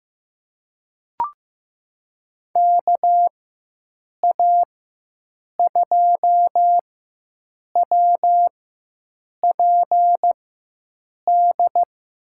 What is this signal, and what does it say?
Morse code, heard as a keyed tone, about 15 wpm, spaced as for 7 wpm, 700 Hz. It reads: KA2WPD
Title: USA Call Signs - 6 Characters - 4x Character Spacing - 15wpm